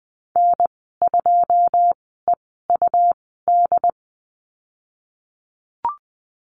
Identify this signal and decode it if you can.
Morse code, heard as a keyed tone, about 20 wpm, spaced as for 15 wpm, 700 Hz. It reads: N2EUD
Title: 154 Lesson 25g - Letter p - Call Signs 20wpm